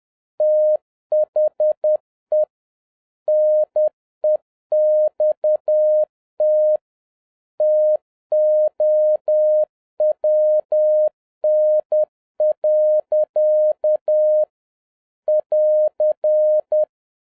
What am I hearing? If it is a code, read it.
Morse code, heard as a keyed tone, about 10 wpm, 610 Hz. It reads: THE NEXT TOWN. <AR>